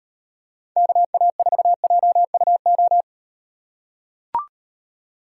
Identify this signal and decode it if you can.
Morse code, heard as a keyed tone, about 38 wpm, 700 Hz. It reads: KA4JUO